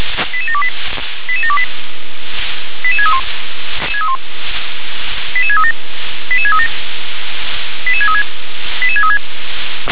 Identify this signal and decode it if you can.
Five-tone selective call (ZVEI), heard as quick runs of five tones, tones of about 70 ms. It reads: 88728 88728 88631 88637 88637 88636 88636